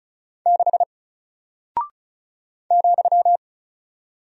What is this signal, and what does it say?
Morse code, heard as a keyed tone, about 34 wpm, 700 Hz. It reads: 6 ,
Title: Single Character (Letter:Number:Punctuation:Prosign) 35wpm